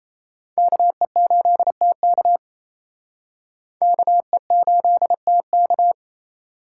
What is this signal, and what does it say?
Morse code, heard as a keyed tone, about 35 wpm, 700 Hz. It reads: KE8TK KE8TK
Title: Speed Racing - USA Call Signs 28wpm